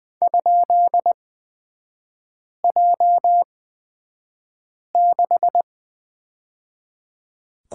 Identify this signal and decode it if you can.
Morse code, heard as a keyed tone, about 20 wpm, spaced as for 5 wpm, 690 Hz. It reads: ?J6
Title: 229 Lesson 37e - Letter x - Three Characters 20wpm